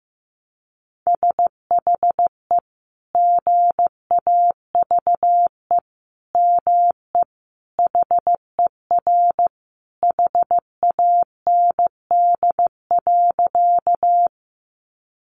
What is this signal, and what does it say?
Morse code, heard as a keyed tone, about 15 wpm, 700 Hz. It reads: SHE GAVE ME HER HAND.
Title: Sentences from Top 500 Words 15wpm